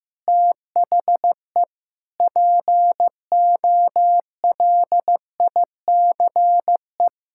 Morse code, heard as a keyed tone, about 15 wpm, 700 Hz. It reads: THE POLICE
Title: Pangram Challenge v2 - 15wpm